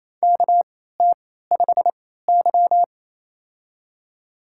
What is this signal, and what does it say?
Morse code, heard as a keyed tone, about 28 wpm, spaced as for 16 wpm, 700 Hz. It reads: KT5Y
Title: USA Call Signs - 3x Character Spacing - 28wpm